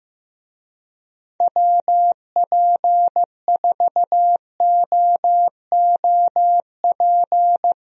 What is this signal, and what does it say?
Morse code, heard as a keyed tone, about 15 wpm, 700 Hz. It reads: WP4OOP
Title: USA Call Signs - 6 Characters - Repeated 3x - 15wpm